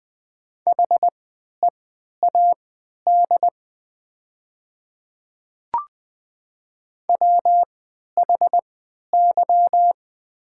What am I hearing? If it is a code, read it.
Morse code, heard as a keyed tone, about 20 wpm, spaced as for 11 wpm, 700 Hz. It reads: HEAD WHY
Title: 3x Character Spacing - Top 500 Words - 20wpm